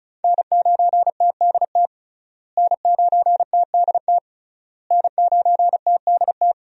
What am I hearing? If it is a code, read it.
Morse code, heard as a keyed tone, about 36 wpm, 700 Hz. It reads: N9TDT N9TDT N9TDT
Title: USA Call Signs - 5 Characters - Repeated 3x - 35wpm